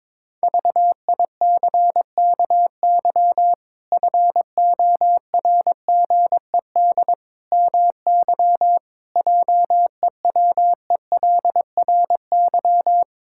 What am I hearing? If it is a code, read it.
Morse code, heard as a keyed tone, about 22 wpm, 700 Hz. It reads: VICKY FORGED MY JEWELRY
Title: Pangram Challenge v2 - 22wpm